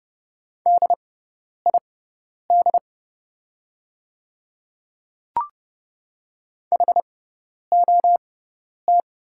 Morse code, heard as a keyed tone, about 29 wpm, spaced as for 10 wpm, 690 Hz. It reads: DID HOT
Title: 6x Character Spacing - Top 100 Words - 30wpm